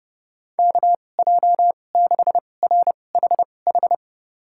Morse code, heard as a keyed tone, about 30 wpm, spaced as for 22 wpm, 700 Hz. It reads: KJ6RHH
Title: USA Call Signs - 6 Characters - 2x Character Spacing - 30wpm